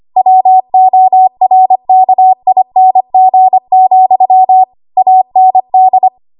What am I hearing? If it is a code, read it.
Morse code, heard as a keyed tone, about 25 wpm, 750 Hz. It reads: WORKING, AND
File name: Archive / 25 WPM / 25WPM